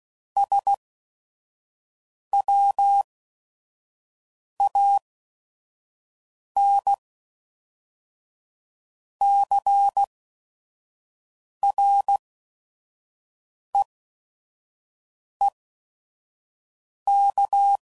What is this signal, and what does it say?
Morse code, heard as a keyed tone, about 16 wpm, spaced as for 5 wpm, 800 Hz. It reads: SWANCREEK
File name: CW Code Groups @5WPM / fromSM's / Cities